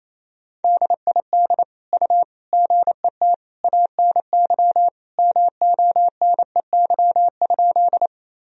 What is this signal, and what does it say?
Morse code, heard as a keyed tone, about 27 wpm, 700 Hz. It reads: DID U GET ANY MONEY?
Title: Sentences from Top 400 Words and Abbreviations 28wpm